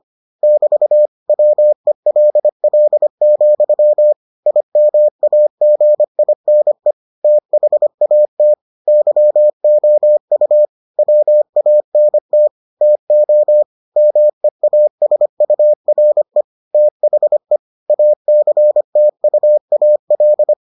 Morse code, heard as a keyed tone, about 25 wpm, 600 Hz. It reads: <BT> WELL, IMAGINE THAT YOU WANT TO MEASURE THE ACTUAL